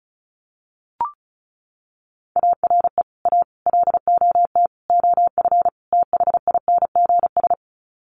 Morse code, heard as a keyed tone, about 34 wpm, 700 Hz. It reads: ARE A LOT OF THINGS
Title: Sets of 5 Words Encore2 35wpm